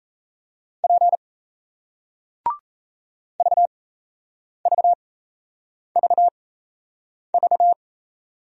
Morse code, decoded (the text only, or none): P VVVV